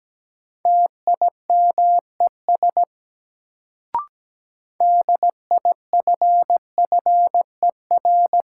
Morse code, heard as a keyed tone, about 17 wpm, 700 Hz. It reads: TIMES DIFFER
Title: Sentences from Top 1100 Words - New Word Drill Alt - 17wpm